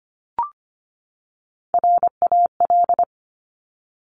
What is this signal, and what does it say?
Morse code, heard as a keyed tone, about 25 wpm, 700 Hz. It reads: RAL
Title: Top Three Letter Combinations in Words 25wpm